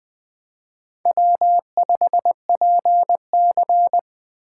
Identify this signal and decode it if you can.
Morse code, heard as a keyed tone, about 24 wpm, 700 Hz. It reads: W5PC
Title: Speed Racing - USA 1x2 Call Signs 20wpm